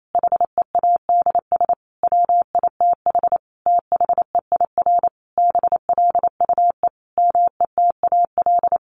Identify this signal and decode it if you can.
Morse code, heard as a keyed tone, about 28 wpm, 700 Hz. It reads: HEADS WITH THEIR BLUE METAL
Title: A Sound Of Thunder 28wpm